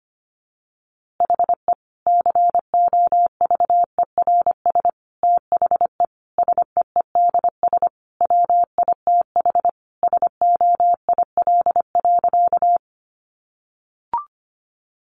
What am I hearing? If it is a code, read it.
Morse code, heard as a keyed tone, about 25 wpm, 700 Hz. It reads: HE COVERS THE SEEDS WITH SOIL.
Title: Easy Phrases 25wpm